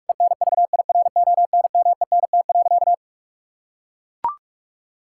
Morse code, heard as a keyed tone, about 43 wpm, 700 Hz. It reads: ENVIRONMENT.
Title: Sentences from Top 1000 Words - Review Part 5 - 45wpm